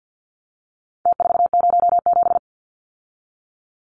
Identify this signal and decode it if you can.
Morse code, heard as a keyed tone, about 50 wpm, 700 Hz. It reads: T407
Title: Numbers - Contest Style - 4-Digit 50wpm